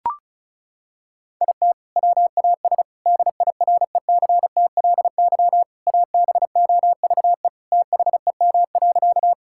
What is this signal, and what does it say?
Morse code, heard as a keyed tone, about 34 wpm, 700 Hz. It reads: IT WAS DIRECTLY ABOVE THEM.
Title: Sentences from Top 1000 Words - Review Part 5 - 35wpm